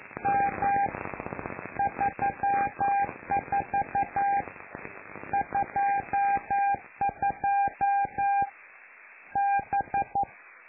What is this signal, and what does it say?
Morse code, heard as a keyed tone, about 11 wpm, 800 Hz. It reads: M 34 22 B
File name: lu4aao / Globo Amsat